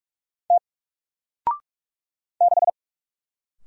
Morse code, heard as a keyed tone, about 44 wpm, 700 Hz. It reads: T 6
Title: Single Letter-Number Alt 45wpm